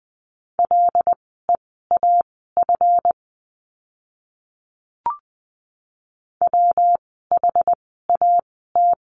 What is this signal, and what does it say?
Morse code, heard as a keyed tone, about 20 wpm, spaced as for 15 wpm, 700 Hz. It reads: LEAF WHAT
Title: 139 Lesson 23f - Letter f - Words 20wpm